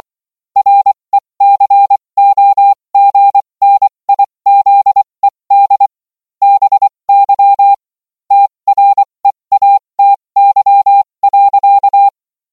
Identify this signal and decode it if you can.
Morse code, heard as a keyed tone, about 24 wpm, spaced as for 20 wpm, 800 Hz. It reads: RECOGNIZED BY TREATY.